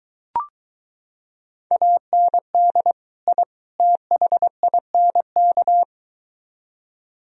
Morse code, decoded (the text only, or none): AND I THINK